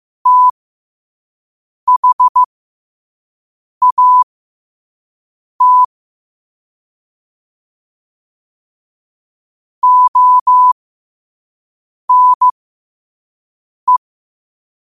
Morse code, heard as a keyed tone, about 15 wpm, spaced as for 5 wpm, 1000 Hz. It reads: THAT ONE
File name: qotdfiles / 5wpm